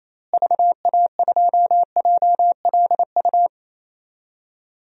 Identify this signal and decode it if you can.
Morse code, heard as a keyed tone, about 28 wpm, 700 Hz. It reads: VA2JLU
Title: DXCC-100 Call Signs - 6 Characters - Repeated 3x - 28wpm